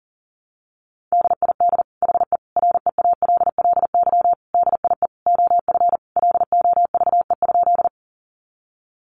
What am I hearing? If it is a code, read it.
Morse code, heard as a keyed tone, about 40 wpm, 700 Hz. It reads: DID HE REALLY DIE OF LOVE?
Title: Sentences from Top 1300 Words 40wpm